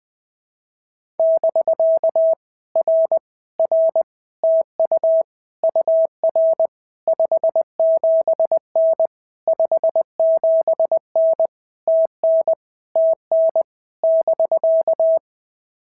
Morse code, heard as a keyed tone, about 20 wpm, 650 Hz. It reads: <BK> R R TU UR 57N 57N TN TN <BK>
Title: POTA QSO Practice - 20wpm